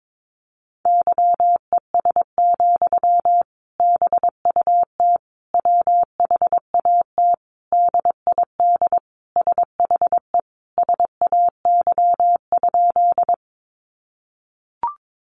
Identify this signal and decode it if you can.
Morse code, heard as a keyed tone, about 22 wpm, 700 Hz. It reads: YES, BUT WHAT DID SHE SAY?
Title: Sentences from Top 500 Words 22wpm